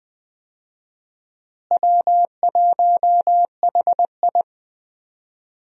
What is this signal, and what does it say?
Morse code, heard as a keyed tone, about 20 wpm, 700 Hz. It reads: W1HI
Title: USA Call Signs - 4 Characters - Repeated 3x - 20wpm